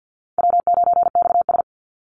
Morse code, heard as a keyed tone, about 50 wpm, 700 Hz. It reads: W9XS
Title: Easy Call Signs - 50wpm